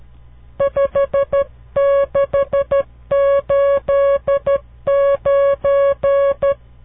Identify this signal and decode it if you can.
Morse code, heard as a keyed tone, about 12 wpm, 560 Hz. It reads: 5689
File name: CW Learning Programs / farnsworth / unit 5